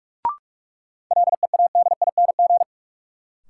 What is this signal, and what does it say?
Morse code, heard as a keyed tone, about 44 wpm, 700 Hz. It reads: LEADING
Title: Sentences from Top 1100 Words - New Word Drill - 45wpm